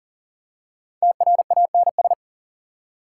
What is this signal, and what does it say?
Morse code, heard as a keyed tone, about 40 wpm, 700 Hz. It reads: TRANS
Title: Prefixes and Suffixes 40wpm